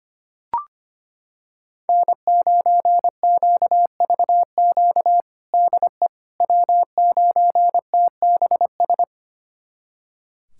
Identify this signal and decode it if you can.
Morse code, heard as a keyed tone, about 25 wpm, 700 Hz. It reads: N9QVQ DE W9TBS
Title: USA Callsign DE Callsign - 5 Characters - 25wpm